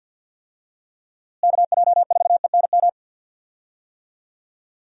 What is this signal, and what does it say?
Morse code, heard as a keyed tone, about 50 wpm, 700 Hz. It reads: KJ4ENM
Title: USA Call Signs - 6 Characters - Repeated 2x - 50wpm